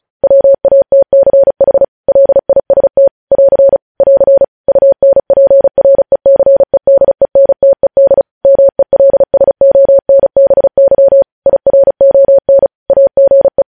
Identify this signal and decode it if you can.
Morse code, heard as a keyed tone, about 36 wpm, 550 Hz. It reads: WATCH LIST <AR> <AR> UNPRECEDENTED MELSONBY IRON AGE
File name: podcast / News headlines-1x-35x35